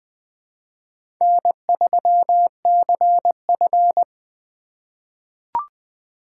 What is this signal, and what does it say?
Morse code, heard as a keyed tone, about 21 wpm, 700 Hz. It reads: N3CF
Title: USA Call Signs - 4 Characters - 3x Character Spacing - 20wpm